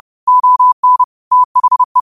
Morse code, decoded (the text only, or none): ON THE